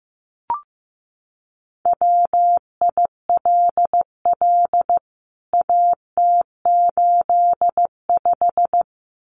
Morse code, heard as a keyed tone, about 15 wpm, 700 Hz. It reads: WILL AT85
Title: CWT Exchange 15wpm